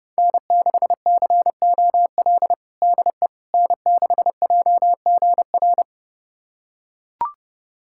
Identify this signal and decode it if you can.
Morse code, heard as a keyed tone, about 30 wpm, 700 Hz. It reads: N6COL DE N6JGR
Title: USA Callsign DE Callsign - 5 Characters - 30wpm